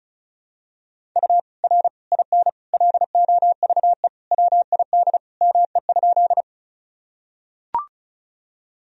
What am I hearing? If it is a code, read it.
Morse code, heard as a keyed tone, about 34 wpm, 700 Hz. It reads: U R IN LOVE WID ME?
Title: Sentences from Top 400 Words and Abbreviations 35wpm